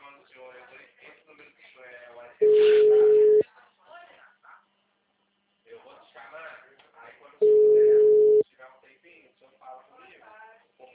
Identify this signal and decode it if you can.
Ringback tone (Europe, 425 Hz), with speech.